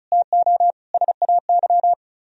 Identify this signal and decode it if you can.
Morse code, heard as a keyed tone, about 46 wpm, spaced as for 41 wpm, 700 Hz. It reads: TO SAY
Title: Speed-Racing - Sets of 5 Words 35wpm